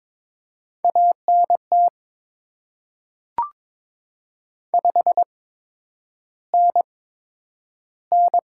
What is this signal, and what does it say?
Morse code, heard as a keyed tone, about 22 wpm, spaced as for 9 wpm, 700 Hz. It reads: ANT 5NN
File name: Top 20 QSO Elements - 8x Character Spacing - 22wpm